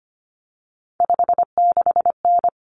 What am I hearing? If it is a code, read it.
Morse code, heard as a keyed tone, about 25 wpm, 700 Hz. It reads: 56N
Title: Common Cut Numbers - 3-Digit 25wpm